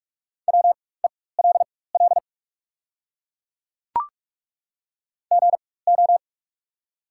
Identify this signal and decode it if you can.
Morse code, heard as a keyed tone, about 43 wpm, spaced as for 21 wpm, 700 Hz. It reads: WELL GO
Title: 4x Character Spacing - Top 300 Words - 45wpm